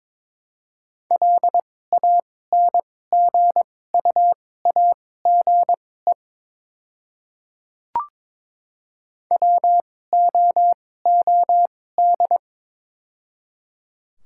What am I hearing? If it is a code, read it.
Morse code, heard as a keyed tone, about 22 wpm, spaced as for 16 wpm, 700 Hz. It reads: LANGUAGE WOOD
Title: 2x Character Spacing - Top 500 Words - 22wpm